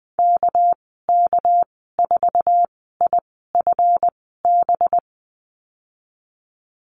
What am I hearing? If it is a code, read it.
Morse code, heard as a keyed tone, about 20 wpm, spaced as for 15 wpm, 700 Hz. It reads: KK4IFB
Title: 207 Lesson 33g - Letter j - Call Signs 20wpm